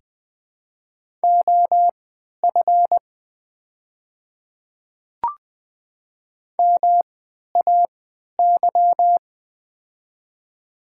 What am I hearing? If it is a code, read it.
Morse code, heard as a keyed tone, about 20 wpm, spaced as for 11 wpm, 700 Hz. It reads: OF MAY
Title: 3x Character Spacing - Top 100 Words - 20wpm